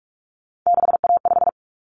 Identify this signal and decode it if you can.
Morse code, heard as a keyed tone, about 44 wpm, 700 Hz. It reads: 6A5
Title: Common Cut Numbers - 3-Digit 45wpm